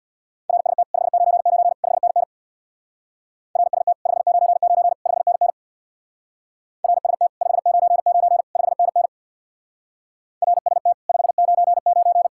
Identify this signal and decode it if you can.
Morse code, heard as a keyed tone, about 66 wpm, 700 Hz. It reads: RST 599 5NN RST 599 5NN RST 599 5NN RST 599